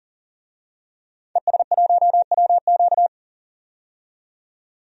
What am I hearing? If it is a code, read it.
Morse code, heard as a keyed tone, about 38 wpm, 700 Hz. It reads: ES1WQ
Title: DXCC-100 Call Signs - 5 Characters - 40wpm